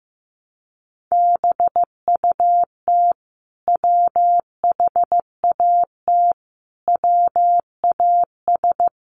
Morse code, heard as a keyed tone, about 15 wpm, 700 Hz. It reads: BUT WHAT WAS
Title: Sentences from Top 1100 Words 15wpm